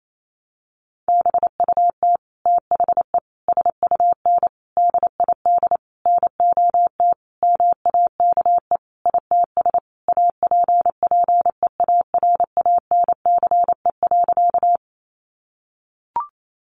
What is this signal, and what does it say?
Morse code, heard as a keyed tone, about 28 wpm, 700 Hz. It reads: BUT THE SUN DID NOT MAKE ITS APPEARANCE.
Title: Sentences from Top 1100 Words 28wpm